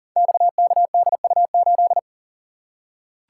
Morse code, heard as a keyed tone, about 39 wpm, 700 Hz. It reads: XKDU8
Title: Code Groups - 5 Letter:Numbers - 40wpm